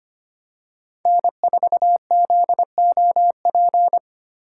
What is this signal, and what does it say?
Morse code, heard as a keyed tone, about 25 wpm, 700 Hz. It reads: N4ZOP
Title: USA Call Signs - 6x Character Spacing - 25wpm